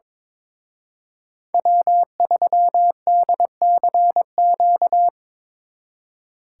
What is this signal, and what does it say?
Morse code, heard as a keyed tone, about 25 wpm, 700 Hz. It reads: W3DCQ
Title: Speed Racing - USA 1x3 Call Signs 22wpm